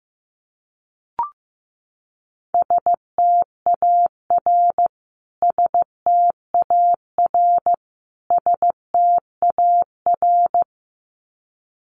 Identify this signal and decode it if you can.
Morse code, heard as a keyed tone, about 15 wpm, 700 Hz. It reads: STAR STAR STAR
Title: Top 500 Words Repeated 15wpm